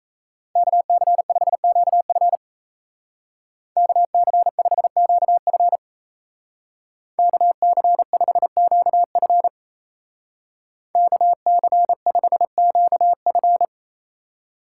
Morse code, heard as a keyed tone, about 34 wpm, 700 Hz. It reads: KC5QF KC5QF KC5QF KC5QF
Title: Speed Racing - USA 2x2 Call Signs 28wpm